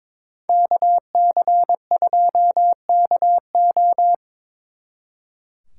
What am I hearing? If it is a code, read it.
Morse code, heard as a keyed tone, about 24 wpm, 700 Hz. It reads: KC2KO
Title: Speed Racing - USA 2x2 Call Signs 22wpm